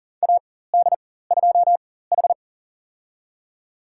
Morse code, heard as a keyed tone, about 40 wpm, spaced as for 19 wpm, 700 Hz. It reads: AD2H
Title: USA Call Signs - 4 Characters - 4x Character Spacing - 40wpm